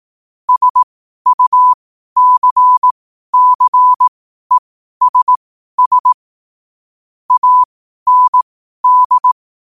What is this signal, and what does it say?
Morse code, heard as a keyed tone, about 18 wpm, spaced as for 12 wpm, 1000 Hz. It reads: SUCCESS AND
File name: qotdfiles / 12wpm